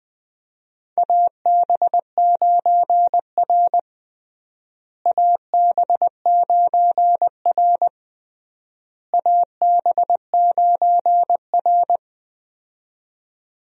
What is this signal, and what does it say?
Morse code, heard as a keyed tone, about 20 wpm, 700 Hz. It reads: AB9R AB9R AB9R
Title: USA Call Signs - 4 Characters - Repeated 3x - 20wpm